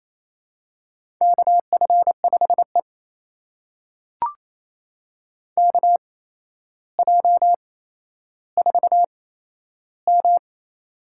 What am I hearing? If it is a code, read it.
Morse code, heard as a keyed tone, about 27 wpm, spaced as for 12 wpm, 700 Hz. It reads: KF5E KJ4M
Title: USA Call Signs - 4 Characters - 8x Character Spacing - 28wpm